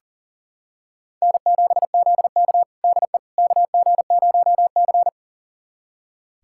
Morse code, heard as a keyed tone, about 40 wpm, 700 Hz. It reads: N7ZK DE KG0C
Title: USA Callsign DE Callsign - 4 Characters - 40wpm